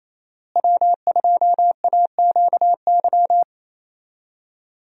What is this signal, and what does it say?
Morse code, heard as a keyed tone, about 27 wpm, 700 Hz. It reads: W2AQY
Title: USA Call Signs - 5 Characters - Repeated 3x - 28wpm